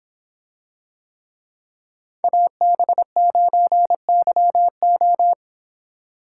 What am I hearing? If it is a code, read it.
Morse code, heard as a keyed tone, about 26 wpm, 700 Hz. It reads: AB9YO